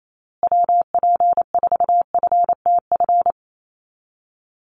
Morse code, heard as a keyed tone, about 28 wpm, 700 Hz. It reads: WP4FTF
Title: USA Call Signs Repeated 3x 28wpm